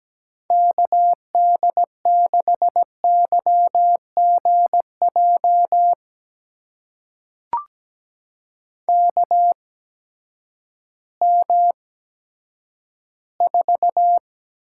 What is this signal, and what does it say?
Morse code, heard as a keyed tone, about 17 wpm, spaced as for 10 wpm, 700 Hz. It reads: KD6YGJ KM4
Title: USA Call Signs - 6 Characters - 8x Character Spacing - 17wpm